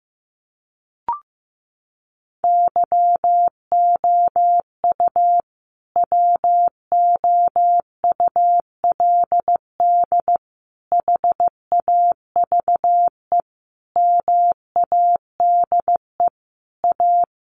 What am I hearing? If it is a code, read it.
Morse code, heard as a keyed tone, about 15 wpm, 700 Hz. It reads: YOU WOULD HAVE MADE A
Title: Sentences from Top 1500 Words 15wpm